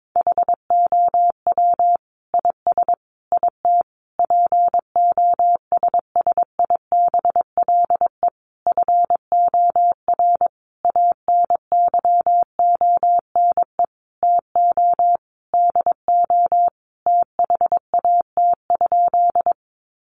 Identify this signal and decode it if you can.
Morse code, heard as a keyed tone, about 22 wpm, 700 Hz. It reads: HOW IS IT POSSIBLE FOR ANYONE TO DO THAT?
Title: Sentences from Top 900 Words 22wpm